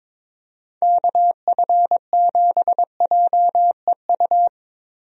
Morse code, heard as a keyed tone, about 22 wpm, 700 Hz. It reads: KF7JEU
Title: USA Call Signs - 6 Characters - Repeated 3x - 22wpm